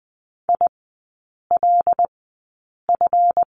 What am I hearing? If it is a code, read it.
Morse code, heard as a keyed tone, about 20 wpm, spaced as for 8 wpm, 700 Hz. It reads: ILF
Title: 172 Lesson 28e - Number 9 - Three Characters 20wpm